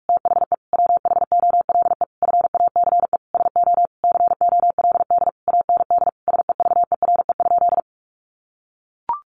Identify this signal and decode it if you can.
Morse code, heard as a keyed tone, about 44 wpm, 700 Hz. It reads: THE WHOLE FACE SO COLD AND SEVERE?
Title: Sentences from Top 1400 Words 45wpm